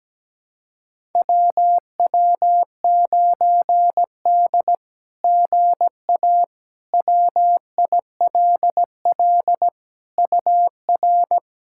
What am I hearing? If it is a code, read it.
Morse code, heard as a keyed tone, about 17 wpm, 700 Hz. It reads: WW9D GA WILL UR
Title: SOTA QSO Practice 17wpm